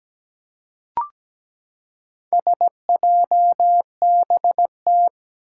Morse code, heard as a keyed tone, about 17 wpm, 700 Hz. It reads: SJBT
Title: Four Challenging Letter-Number Combinations 17wpm